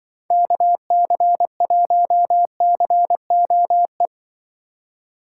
Morse code, heard as a keyed tone, about 23 wpm, 700 Hz. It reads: KC1COE
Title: Speed Racing - USA Call Signs 20wpm